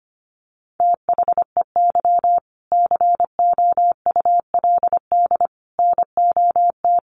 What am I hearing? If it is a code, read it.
Morse code, heard as a keyed tone, about 25 wpm, 700 Hz. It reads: THEY COULD NOT
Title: Sentences from Top 1400 Words 25wpm